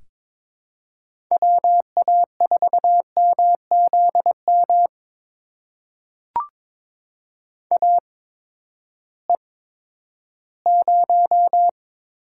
Morse code, heard as a keyed tone, about 22 wpm, spaced as for 12 wpm, 700 Hz. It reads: WA4MZM AE0
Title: USA Call Signs - 8x Character Spacing - 22wpm